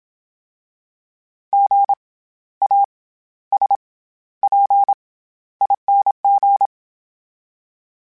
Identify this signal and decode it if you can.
Morse code, heard as a keyed tone, about 27 wpm, 800 Hz. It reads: G A S P ING
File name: Special INGs-mp3s / Special INGs-mp3s / SlowToFaster ing @25